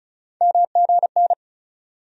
Morse code, heard as a keyed tone, about 36 wpm, 700 Hz. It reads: MGN
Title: USA Call Sign Postfixes - 3 Letters 35wpm